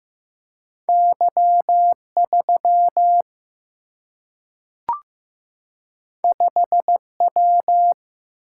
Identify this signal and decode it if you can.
Morse code, heard as a keyed tone, about 15 wpm, 700 Hz. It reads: Y3 5W
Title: Single-Letter Single-Number 15wpm